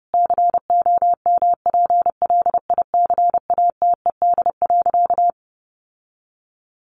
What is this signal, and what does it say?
Morse code, heard as a keyed tone, about 30 wpm, 700 Hz. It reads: COMPLICATED.
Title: Sentences from Top 1200 Words 30wpm